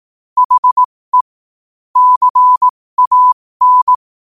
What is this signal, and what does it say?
Morse code, heard as a keyed tone, about 18 wpm, spaced as for 15 wpm, 1000 Hz. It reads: HE CAN
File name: qotdfiles / 15wpm